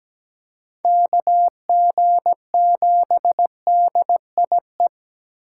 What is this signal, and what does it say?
Morse code, heard as a keyed tone, about 20 wpm, 700 Hz. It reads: KG7DIE
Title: Speed Racing - USA 2x3 Call Signs 17wpm